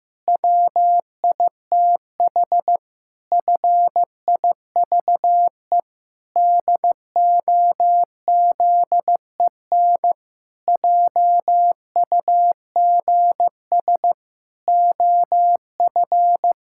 Morse code, heard as a keyed tone, about 15 wpm, 700 Hz. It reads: WITH FIVE DOZEN JUGS OF
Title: Pangram Challenge v2 - 15wpm